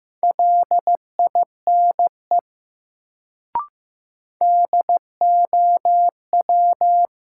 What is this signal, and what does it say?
Morse code, heard as a keyed tone, about 15 wpm, 700 Hz. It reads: LINE DOW
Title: Sets of 5 Words - New Word Drill - 15wpm